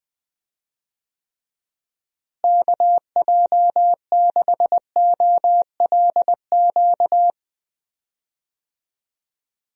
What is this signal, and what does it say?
Morse code, heard as a keyed tone, about 20 wpm, 700 Hz. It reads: KJ6OLQ
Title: USA Call Signs - 6 Characters - Repeated 2x - 20wpm